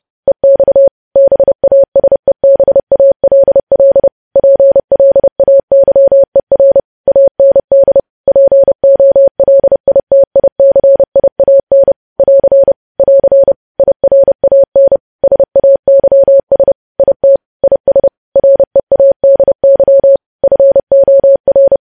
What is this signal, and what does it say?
Morse code, heard as a keyed tone, about 30 wpm, 550 Hz. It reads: EX BASEBALL PLAYER AND POLITICIAN <AR> <AR> IRAN SAYS IT IS READY FOR
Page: news headlines in morse code at 30 wpm